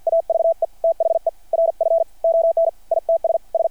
Morse code, heard as a keyed tone, about 53 wpm, 660 Hz. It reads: AVE THE RF ON ITS S
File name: archived / CRSnet